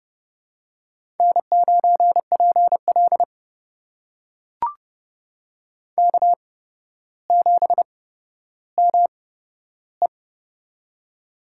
Morse code, heard as a keyed tone, about 29 wpm, spaced as for 13 wpm, 700 Hz. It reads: N9PL K7ME
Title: USA Call Signs - 4 Characters - 8x Character Spacing - 30wpm